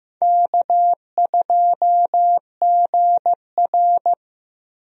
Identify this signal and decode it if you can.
Morse code, heard as a keyed tone, about 15 wpm, 700 Hz. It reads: K2GR
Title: USA Call Signs - 4 Characters - Repeated 2x - 15wpm